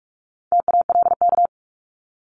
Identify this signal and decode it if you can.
Morse code, heard as a keyed tone, about 46 wpm, 700 Hz. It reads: TALK
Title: Multi-letter Phonemes 45wpm